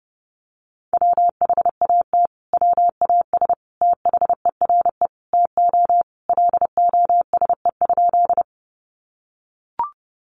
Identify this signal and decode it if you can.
Morse code, heard as a keyed tone, about 30 wpm, 700 Hz. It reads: WHAT WAS THERE TO LOSE?
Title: Sentences from Top 1100 Words 30wpm